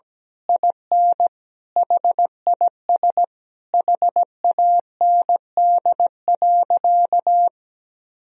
Morse code, heard as a keyed tone, about 17 wpm, 700 Hz. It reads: IN HIS HAND.
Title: Sentences from Top 1000 Words - Review Part 2 - 17wpm